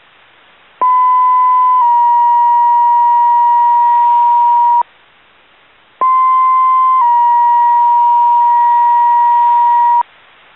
Two-tone page, heard as a one-second tone, then a three-second tone, about 1010 then 950 Hz, about 1040 then 950 Hz. It is heard from 0.8 to 10.0 s.